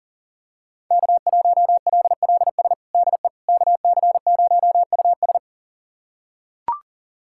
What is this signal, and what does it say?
Morse code, heard as a keyed tone, about 40 wpm, 700 Hz. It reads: K1LLS DE KC0US
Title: USA Callsign DE Callsign - 5 Characters - 40wpm